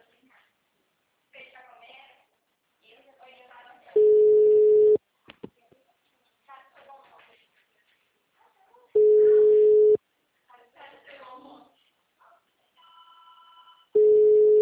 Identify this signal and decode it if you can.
Ringback tone (Europe, 425 Hz), with speech.
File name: out- -2003- - - .102